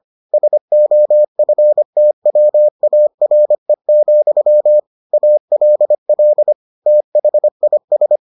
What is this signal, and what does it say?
Morse code, heard as a keyed tone, about 25 wpm, 600 Hz. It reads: SOFTWARE, ALL THIS